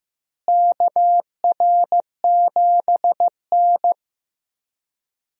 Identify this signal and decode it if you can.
Morse code, heard as a keyed tone, about 15 wpm, 700 Hz. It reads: KR7N
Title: USA Call Signs - 4 Characters - Repeated 2x - 15wpm